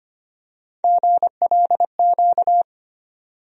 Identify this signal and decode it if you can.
Morse code, heard as a keyed tone, about 25 wpm, 700 Hz. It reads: GLQ
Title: USA Call Sign Postfixes - 3 Letters 25wpm